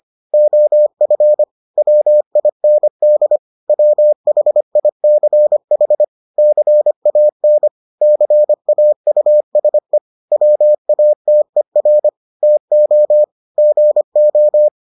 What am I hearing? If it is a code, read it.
Morse code, heard as a keyed tone, about 25 wpm, 600 Hz. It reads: OF WIND WHICH CAN CAUSE WATER TO GO